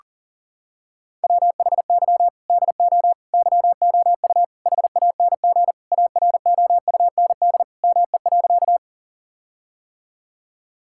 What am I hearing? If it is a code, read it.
Morse code, heard as a keyed tone, about 40 wpm, 700 Hz. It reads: WHY DO YOU HANG AROUND ME.
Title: Sentences from Top 1600 Words 40wpm